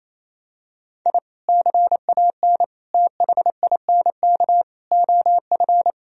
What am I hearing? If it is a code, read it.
Morse code, heard as a keyed tone, about 28 wpm, 700 Hz. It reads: I CAN THINK OF
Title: Sentences from Top 1000 Words - Review Part 1 - 28wpm